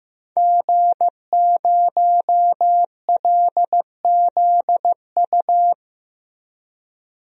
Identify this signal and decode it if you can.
Morse code, heard as a keyed tone, about 15 wpm, 700 Hz. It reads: G0LZU
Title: DXCC-100 Call Signs - 5 Characters - Repeated 3x - 15wpm